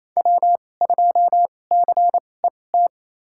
Morse code, heard as a keyed tone, about 27 wpm, spaced as for 20 wpm, 700 Hz. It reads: W2CET
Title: USA Call Signs - 5 Characters - 2x Character Spacing - 28wpm